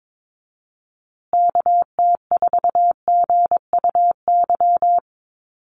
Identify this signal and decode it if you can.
Morse code, heard as a keyed tone, about 22 wpm, 700 Hz. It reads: KT4GUY
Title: USA Call Signs Repeated 2x 22wpm